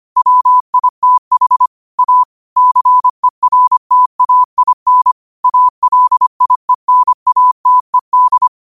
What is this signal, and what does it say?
Morse code, heard as a keyed tone, about 25 wpm, 1000 Hz. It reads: WITH A CERTAIN ALIENATED
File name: qotdfiles / 25wpm